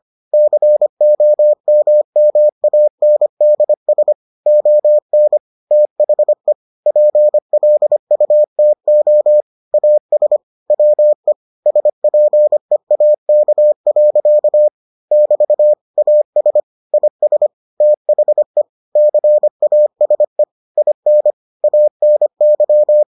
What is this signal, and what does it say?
Morse code, heard as a keyed tone, about 25 wpm, 600 Hz. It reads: COMMANDS ON THE PLUTO AS WE SPEAK. <BT> AS IS THE CASE IN ANY